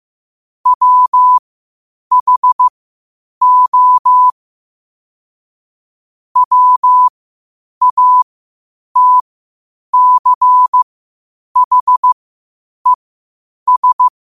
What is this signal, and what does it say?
Morse code, heard as a keyed tone, about 15 wpm, spaced as for 8 wpm, 1000 Hz. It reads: WHO WATCHES
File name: qotdfiles / 8wpm